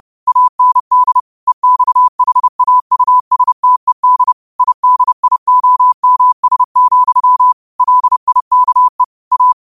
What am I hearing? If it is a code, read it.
Morse code, heard as a keyed tone, about 30 wpm, 1000 Hz. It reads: AND EXHAUSTED IDIOMS, LIKE A